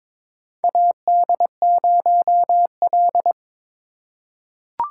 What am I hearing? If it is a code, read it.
Morse code, heard as a keyed tone, about 22 wpm, 700 Hz. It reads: AD0L
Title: USA Call Signs - 4 Characters - 4x Character Spacing - 22wpm